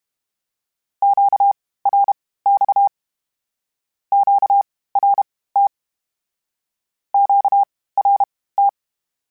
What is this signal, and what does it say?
Morse code, heard as a keyed tone, about 32 wpm, spaced as for 18 wpm, 800 Hz. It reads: QRX QRT QRT
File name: (spaced) / Q-Calls Spaced-mp3s / Q Calls Twice sp @30